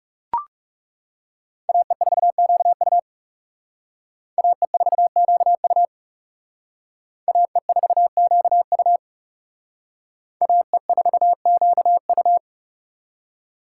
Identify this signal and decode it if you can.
Morse code, heard as a keyed tone, about 39 wpm, 700 Hz. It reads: AE4QU AE4QU AE4QU AE4QU